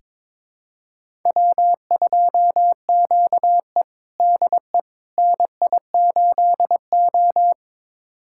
Morse code, heard as a keyed tone, about 22 wpm, 700 Hz. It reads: W2QE DE NI8O
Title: USA Callsign DE Callsign - 4 Characters - 22wpm